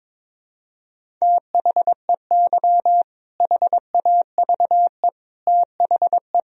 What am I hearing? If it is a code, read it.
Morse code, heard as a keyed tone, about 22 wpm, 700 Hz. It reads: THEY HAVE THE
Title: Sentences from Top 1000 Words - Review Part 3 - 22wpm